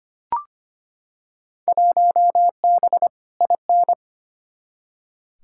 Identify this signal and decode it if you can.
Morse code, heard as a keyed tone, about 25 wpm, 700 Hz. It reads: 1B IN
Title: Field Day Exchanges - 25wpm